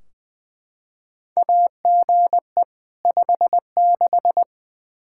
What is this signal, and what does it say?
Morse code, heard as a keyed tone, about 20 wpm, 700 Hz. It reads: AGE 56
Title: K1USN SST QSO Practice 20wpm